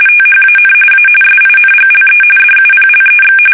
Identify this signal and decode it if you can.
RTTY, about 45.45 baud. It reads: THAN A SINGLE DUPLEX
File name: fsk850